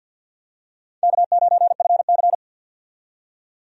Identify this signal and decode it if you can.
Morse code, heard as a keyed tone, about 52 wpm, 700 Hz. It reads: K9FC
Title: USA Call Signs - 4 Characters - 50wpm